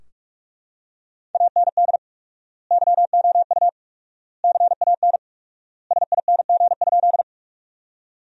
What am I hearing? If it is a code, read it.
Morse code, heard as a keyed tone, about 45 wpm, 700 Hz. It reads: AND YOU CAN SING?
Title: 4x Word-Spacing - Sentences from Top 400 Words - 45wpm